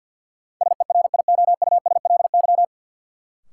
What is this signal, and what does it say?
Morse code, heard as a keyed tone, about 50 wpm, 700 Hz. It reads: SERIOUSLY
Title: Sentences from Top 1200 Words - New Word Drill Alt - 50wpm